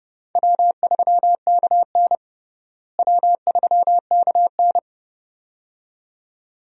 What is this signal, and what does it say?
Morse code, heard as a keyed tone, about 30 wpm, 700 Hz. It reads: W3KN W3KN
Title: USA Call Signs - 4 Characters - Repeated 2x - 30wpm